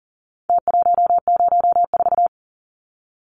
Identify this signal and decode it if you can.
Morse code, heard as a keyed tone, about 40 wpm, 700 Hz. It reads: T104
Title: Numbers - Contest Style - 4-Digit 40wpm